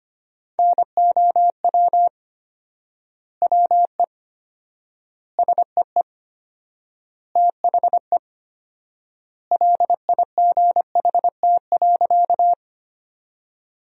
Morse code, heard as a keyed tone, about 25 wpm, 700 Hz. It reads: NOW WE SEE THE LIGHT.
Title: 4x Word-Spacing - Sentences from Top 200 Words - 25wpm